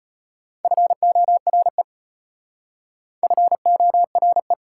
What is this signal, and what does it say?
Morse code, heard as a keyed tone, about 33 wpm, 700 Hz. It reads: FORE FORE